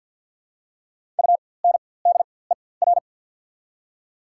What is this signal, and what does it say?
Morse code, heard as a keyed tone, about 54 wpm, spaced as for 24 wpm, 690 Hz. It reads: UNDER
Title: 4x Character Spacing - Top 300 Words - 50wpm